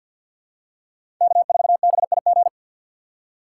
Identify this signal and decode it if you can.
Morse code, heard as a keyed tone, about 51 wpm, 690 Hz. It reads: K4BIG
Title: USA Call Signs - 5 Characters - 4x Character Spacing - 50wpm